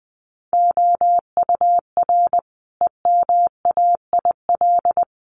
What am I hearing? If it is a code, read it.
Morse code, heard as a keyed tone, about 20 wpm, 700 Hz. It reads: OUR EMAIL
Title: Sets of 5 Words Encore2 20wpm